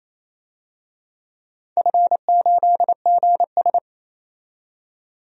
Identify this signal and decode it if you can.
Morse code, heard as a keyed tone, about 28 wpm, 700 Hz. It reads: F8GS
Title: DXCC-100 Call Signs - 4 Characters - 28wpm